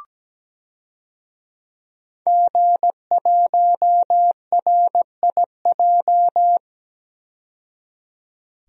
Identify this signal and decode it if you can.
Morse code, heard as a keyed tone, about 17 wpm, 700 Hz. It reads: G1RIJ
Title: DXCC-100 Call Signs - 5 Characters - 17wpm